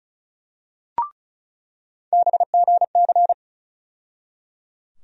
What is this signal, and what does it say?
Morse code, heard as a keyed tone, about 36 wpm, 700 Hz. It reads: BGC
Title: USA Call Sign Postfixes - 3 Letters 35wpm